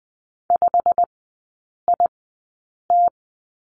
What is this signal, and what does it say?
Morse code, heard as a keyed tone, about 20 wpm, spaced as for 8 wpm, 690 Hz. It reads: 5IT
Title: 093 Lesson 16e - Letter c - Three Characters 20wpm